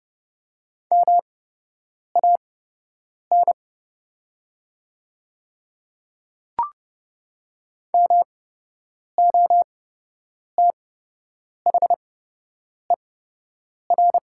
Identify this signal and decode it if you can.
Morse code, heard as a keyed tone, about 31 wpm, spaced as for 8 wpm, 700 Hz. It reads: MAN MOTHER
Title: 8x Character Spacing - Top 200 Words - 30wpm